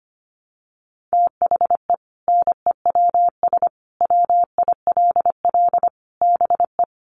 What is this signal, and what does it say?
Morse code, heard as a keyed tone, about 25 wpm, 700 Hz. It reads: THE NEWS WILL BE
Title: Sentences from Top 1200 Words 25wpm